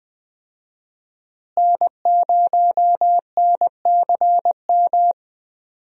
Morse code, heard as a keyed tone, about 20 wpm, 700 Hz. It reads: N0NCM
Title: USA Call Signs - 5 Characters - Repeated 3x - 20wpm